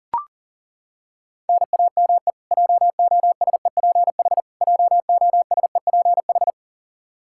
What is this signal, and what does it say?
Morse code, heard as a keyed tone, about 40 wpm, 700 Hz. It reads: NAME JOSEPH JOSEPH
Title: Practice QSO - Exchange 1 - Beginning - 40wpm